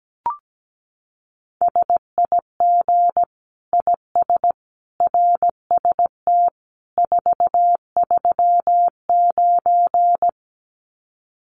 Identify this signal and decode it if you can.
Morse code, heard as a keyed tone, about 17 wpm, 700 Hz. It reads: SIG IS RST 439
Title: Common QSO Phrases 17wpm